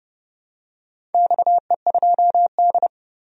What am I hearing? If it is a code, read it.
Morse code, heard as a keyed tone, about 28 wpm, 700 Hz. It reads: XE2D
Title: DXCC-100 Call Signs - 4 Characters - 30wpm